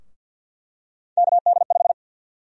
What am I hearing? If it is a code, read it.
Morse code, heard as a keyed tone, about 51 wpm, 700 Hz. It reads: KD5
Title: USA Call Sign Prefixes - 50wpm